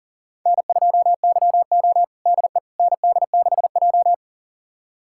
Morse code, heard as a keyed tone, about 37 wpm, 700 Hz. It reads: N2YO DE ND6J
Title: USA Callsign DE Callsign - 4 Characters - 40wpm